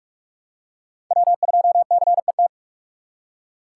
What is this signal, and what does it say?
Morse code, heard as a keyed tone, about 46 wpm, 700 Hz. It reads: W2CET
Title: USA Call Signs - 5 Characters - 4x Character Spacing - 45wpm